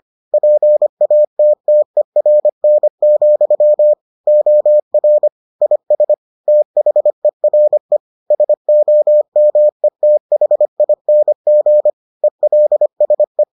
Morse code, heard as a keyed tone, about 25 wpm, 600 Hz. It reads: PATTERN, OR IS THERE SOMETHING ELSE